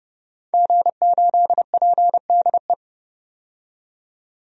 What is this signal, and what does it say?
Morse code, heard as a keyed tone, about 30 wpm, 700 Hz. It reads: G8PDE
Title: DXCC-100 Call Signs - 5 Characters - 30wpm